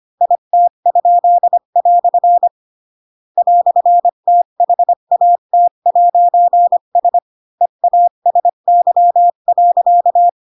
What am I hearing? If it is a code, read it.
Morse code, heard as a keyed tone, about 25 wpm, 700 Hz. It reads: IT?" "THAT'S EASY.
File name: Pinocchio / Pinocchio ch8 20 25 ik1hge